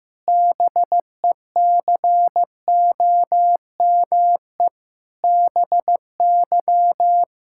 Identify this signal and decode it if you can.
Morse code, heard as a keyed tone, about 15 wpm, 700 Hz. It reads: BECOME BY
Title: Pangram Challenge v2 - 15wpm